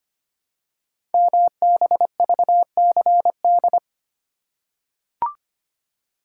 Morse code, heard as a keyed tone, about 25 wpm, 700 Hz. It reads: MBVCD
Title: Code Groups - 5 Letters - 25wpm